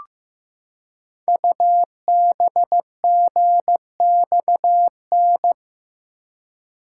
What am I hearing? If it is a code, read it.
Morse code, heard as a keyed tone, about 15 wpm, 700 Hz. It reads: UBGXN
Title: Code Groups - 5 Letters - 15wpm